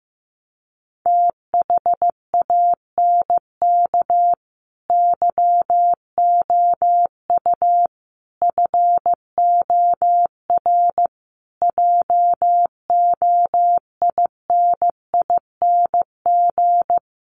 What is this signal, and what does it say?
Morse code, heard as a keyed tone, about 15 wpm, 700 Hz. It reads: THANK YOU FOR JOINING
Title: Sets of 5 Words 15wpm